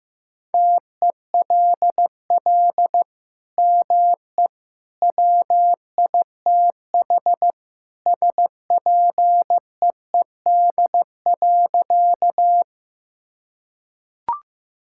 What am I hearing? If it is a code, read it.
Morse code, heard as a keyed tone, about 15 wpm, 700 Hz. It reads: TELL ME WITH SPEED.
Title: Sentences from Top 1000 Words - Review Part 5 - 15wpm